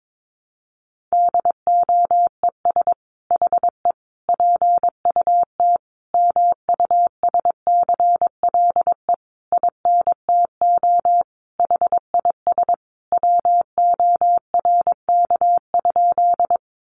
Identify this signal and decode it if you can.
Morse code, heard as a keyed tone, about 22 wpm, 700 Hz. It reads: DOES HE PUT MUSCLE INTO HIS WORK?
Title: Sentences from Top 1300 Words 22wpm